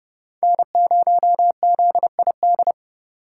Morse code, heard as a keyed tone, about 28 wpm, 700 Hz. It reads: N0ZID